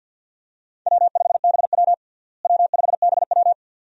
Morse code, heard as a keyed tone, about 53 wpm, 700 Hz. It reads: W5BW W5BW
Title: USA Call Signs - 4 Characters - Repeated 2x - 50wpm